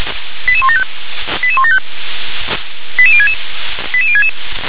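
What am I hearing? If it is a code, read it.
Five-tone selective call (ZVEI), heard as quick runs of five tones, tones of about 70 ms. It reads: 88175 88175 88066 88066
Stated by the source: ZVEI five-tone selective call received on an SDR.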